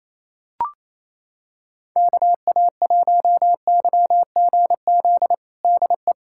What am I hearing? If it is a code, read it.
Morse code, heard as a keyed tone, about 28 wpm, 700 Hz. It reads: KA1YGZ DE
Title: USA Callsign DE Callsign - 6 Characters - 28wpm